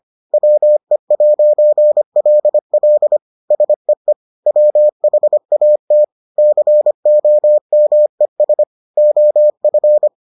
Morse code, heard as a keyed tone, about 25 wpm, 600 Hz. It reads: WE'LL SEE WHAT COMES OF